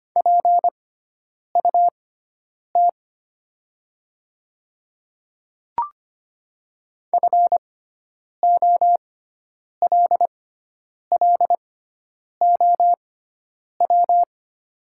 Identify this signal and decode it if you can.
Morse code, heard as a keyed tone, about 25 wpm, spaced as for 9 wpm, 700 Hz. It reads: PUT FOLLOW
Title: 6x Character Spacing - Top 300 Words - 25wpm